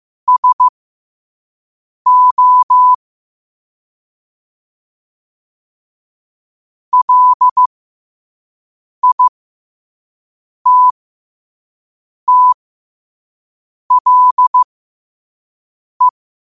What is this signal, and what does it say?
Morse code, heard as a keyed tone, about 15 wpm, spaced as for 5 wpm, 1000 Hz. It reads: SO LITTLE
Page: Quotes for Tue, 12 Aug 2025 in Morse Code at 5 words per minute.